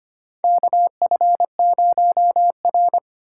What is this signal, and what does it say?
Morse code, heard as a keyed tone, about 25 wpm, 700 Hz. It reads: KF0R
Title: USA Call Signs - 4 Characters - 4x Character Spacing - 25wpm